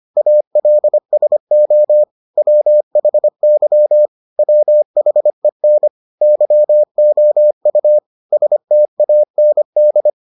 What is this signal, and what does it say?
Morse code, heard as a keyed tone, about 25 wpm, 600 Hz. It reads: ALSO WHY WHEN YOU STAND